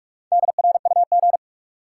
Morse code, heard as a keyed tone, about 43 wpm, 700 Hz. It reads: DRUG
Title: Sets of 5 Words - New Word Drill - 45wpm